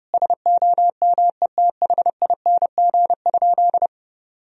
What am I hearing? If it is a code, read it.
Morse code, heard as a keyed tone, about 30 wpm, 700 Hz. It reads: SOMETHING?
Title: Sentences from Top 1000 Words - Review Part 2 - 30wpm